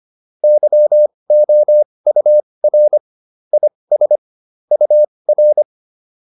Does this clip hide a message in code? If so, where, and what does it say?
Yes: Morse code from 0.4 to 5.6 s, heard as a keyed tone, about 25 wpm, spaced as for 20 wpm, 600 Hz. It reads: YOUR IS UR